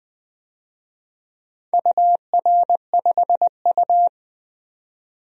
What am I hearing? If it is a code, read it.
Morse code, heard as a keyed tone, about 20 wpm, 700 Hz. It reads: UR5U
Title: DXCC-100 Call Signs - 4 Characters - 20wpm